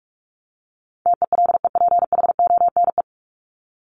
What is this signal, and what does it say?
Morse code, heard as a keyed tone, about 44 wpm, 700 Hz. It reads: TELEPHONE
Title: Top 1501-1750 Words 45wpm